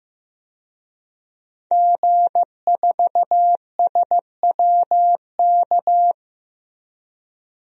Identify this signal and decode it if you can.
Morse code, heard as a keyed tone, about 15 wpm, 700 Hz. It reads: G4SWK
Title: DXCC-100 Call Signs - 5 Characters - Repeated 3x - 15wpm